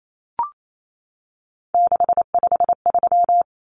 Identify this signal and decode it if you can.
Morse code, heard as a keyed tone, about 28 wpm, 700 Hz. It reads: 653
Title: Common Cut Numbers - 3-Digit 28wpm